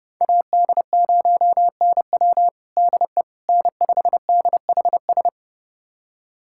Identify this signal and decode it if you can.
Morse code, heard as a keyed tone, about 30 wpm, 700 Hz. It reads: AD0NW DE N5DHS
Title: USA Callsign DE Callsign - 5 Characters - 30wpm